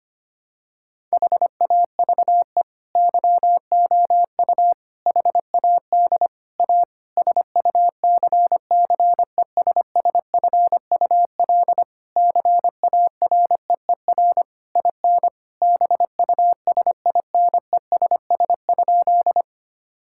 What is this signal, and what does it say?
Morse code, heard as a keyed tone, about 25 wpm, 700 Hz. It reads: HAVE YOU HAD A SUCCESSFUL CAREER IN BUSINESS?
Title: Sentences from Top 1000 Words - Review Part 6 - 25wpm